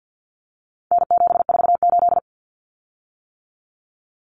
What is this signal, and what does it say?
Morse code, heard as a keyed tone, about 50 wpm, 700 Hz. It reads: N748
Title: Common Cut Numbers - 4-Digit 50wpm